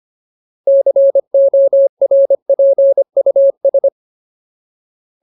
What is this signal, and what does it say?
Morse code, heard as a keyed tone, about 25 wpm, 550 Hz. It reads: CORPUS
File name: Ultimate Word List / 25WPM Ultimate